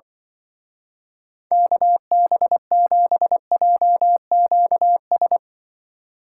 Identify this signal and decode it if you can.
Morse code, heard as a keyed tone, about 25 wpm, 700 Hz. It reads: KB7JQS
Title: Speed Racing - USA Call Signs 20wpm